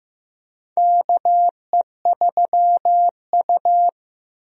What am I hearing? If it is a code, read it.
Morse code, heard as a keyed tone, about 15 wpm, 700 Hz. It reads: KE3U
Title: USA Call Signs - 4 Characters - 15wpm